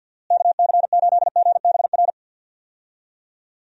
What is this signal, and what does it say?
Morse code, heard as a keyed tone, about 50 wpm, 700 Hz. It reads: KC8GBR
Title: USA Call Signs - 6 Characters - Repeated 3x - 50wpm